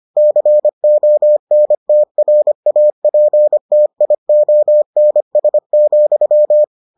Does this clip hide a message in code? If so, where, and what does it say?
Yes: Morse code from 0.2 to 6.6 s, heard as a keyed tone, about 25 wpm, 600 Hz. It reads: CONTRAPTIONS,